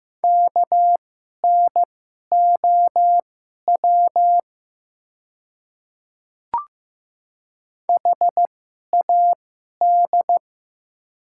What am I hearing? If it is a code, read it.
Morse code, heard as a keyed tone, about 15 wpm, spaced as for 11 wpm, 700 Hz. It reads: KNOW HAD
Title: 2x Character Spacing - Top 100 Words - 15wpm